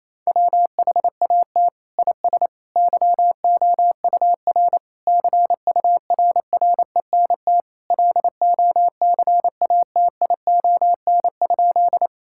Morse code, heard as a keyed tone, about 28 wpm, 700 Hz. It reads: WHAT IS YOUR CURRENT LOCATION?
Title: Sentences from Top 1000 Words - Review Part 6 - 28wpm